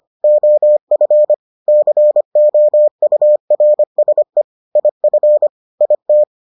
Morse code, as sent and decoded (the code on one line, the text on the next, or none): --- ..-. / -.-. --- ..- .-. ... / .. ..-. / .. -
OF COURSE IF IT